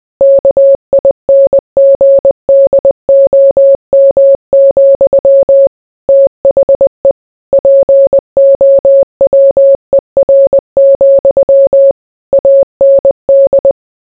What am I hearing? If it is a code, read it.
Morse code, heard as a keyed tone, about 20 wpm, 550 Hz. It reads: KINGDOM, THE POWER, AND